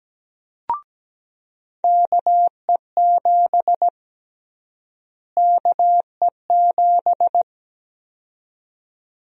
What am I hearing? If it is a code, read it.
Morse code, heard as a keyed tone, about 17 wpm, 700 Hz. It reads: KE7 KE7
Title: USA Call Sign Prefixes - Repeated 2x - 17wpm